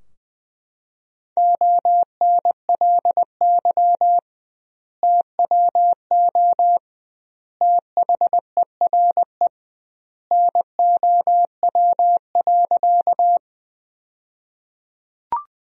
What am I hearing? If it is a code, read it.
Morse code, heard as a keyed tone, about 20 wpm, 700 Hz. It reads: ONLY TWO THERE NOW.
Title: 2x Word-Spacing - Sentences from Top 200 Words - 20wpm